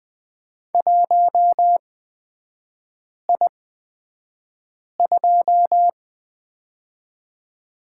Morse code, heard as a keyed tone, about 20 wpm, spaced as for 5 wpm, 690 Hz. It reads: 1I2
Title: 229 Lesson 37e - Letter x - Three Characters 20wpm